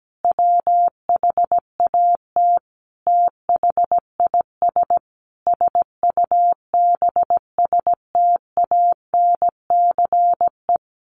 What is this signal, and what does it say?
Morse code, heard as a keyed tone, about 17 wpm, 700 Hz. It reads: WHAT THIS SUBSTANCE
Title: Sentences from Top 1500 Words 17wpm